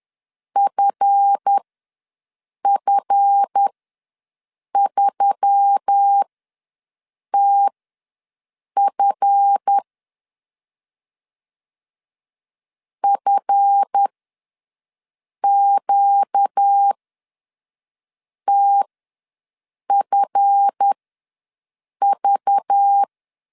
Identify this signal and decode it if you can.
Morse code, heard as a keyed tone, about 10 wpm, spaced as for 5 wpm, 800 Hz. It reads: FF3TF FQTFV
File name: LEZIONI / 7 GF9Z / 7 GF ws 10wpm 3918